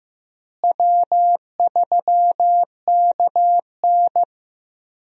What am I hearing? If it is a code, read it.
Morse code, heard as a keyed tone, about 15 wpm, 700 Hz. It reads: W3KN
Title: USA Call Signs - 4 Characters - Repeated 2x - 15wpm